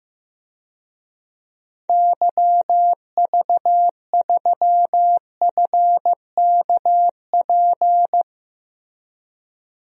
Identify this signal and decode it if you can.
Morse code, heard as a keyed tone, about 15 wpm, 700 Hz. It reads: YV3FKP
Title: DXCC-100 Call Signs - 6 Characters - Repeated 2x - 15wpm